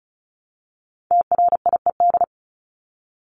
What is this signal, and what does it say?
Morse code, heard as a keyed tone, about 34 wpm, 700 Hz. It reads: TRIED
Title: ED Suffix - Top 100 Words with Base Words 35wpm